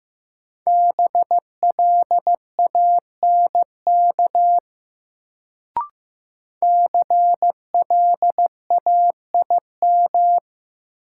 Morse code, heard as a keyed tone, about 15 wpm, 700 Hz. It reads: BLANK CLAIM
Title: Sentences from Top 1500 Words - New Word Drill - 15wpm